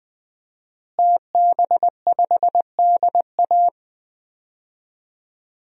Morse code, heard as a keyed tone, about 20 wpm, 700 Hz. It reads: TB5DA
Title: DXCC-100 Call Signs - 5 Characters - Repeated 2x - 20wpm